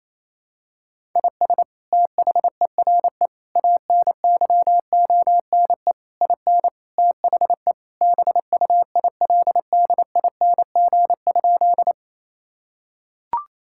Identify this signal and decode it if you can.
Morse code, heard as a keyed tone, about 28 wpm, 700 Hz. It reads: IS THERE ANYONE IN THE BUILDING?
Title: Sentences from Top 1000 Words - Review Part 5 - 28wpm